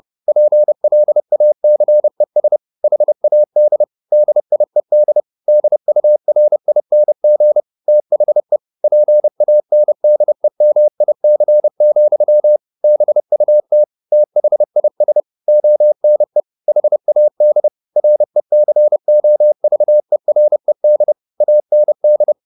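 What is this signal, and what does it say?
Morse code, heard as a keyed tone, about 30 wpm, 600 Hz. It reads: PLACES HAD DIED DURING THE PANDEMIC, BUT THIS ONE HAD RECOVERED AND
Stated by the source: Morse code at 30 wpm, the novel Never by Ken Follett in English, from CW med Gnister.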